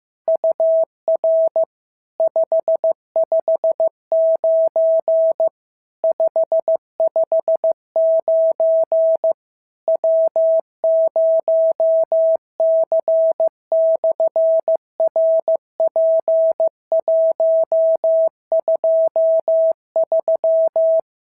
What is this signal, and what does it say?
Morse code, heard as a keyed tone, about 14 wpm, 650 Hz. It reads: UR 559 559 W0C/RP123
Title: SOTA QSO Practice 15wpm